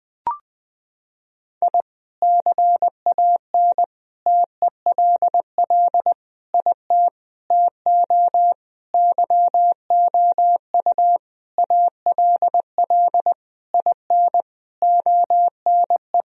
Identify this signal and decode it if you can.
Morse code, heard as a keyed tone, about 20 wpm, 700 Hz. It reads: I CAN TELL IT TO YOU ALL IN ONE
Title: Sentences from Top 1000 Words - Review Part 1 - 20wpm